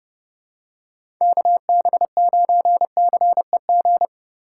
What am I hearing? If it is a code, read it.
Morse code, heard as a keyed tone, about 32 wpm, 700 Hz. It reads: KB9CEG
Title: USA Call Signs - 6 Characters - 2x Character Spacing - 30wpm